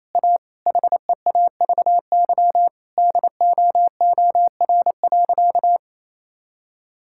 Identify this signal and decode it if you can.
Morse code, heard as a keyed tone, about 28 wpm, 700 Hz. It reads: A HEAVY DOOR.
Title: Sentences from Top 1600 Words 28wpm